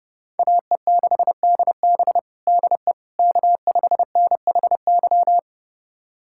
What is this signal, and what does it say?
Morse code, heard as a keyed tone, about 32 wpm, 700 Hz. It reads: AE6DB DE K5NHY
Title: USA Callsign DE Callsign - 5 Characters - 30wpm